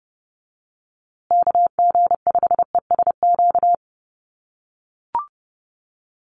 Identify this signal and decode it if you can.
Morse code, heard as a keyed tone, about 30 wpm, 700 Hz. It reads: KG5ESQ
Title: USA Call Signs Repeated 3x 30wpm